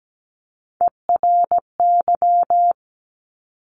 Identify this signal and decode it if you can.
Morse code, heard as a keyed tone, about 17 wpm, 700 Hz. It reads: ERY
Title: Top Three Letter Combinations in Words 17wpm